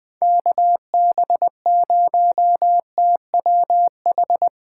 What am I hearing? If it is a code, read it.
Morse code, heard as a keyed tone, about 21 wpm, 700 Hz. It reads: KB0TWH
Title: Speed Racing - USA Call Signs 17wpm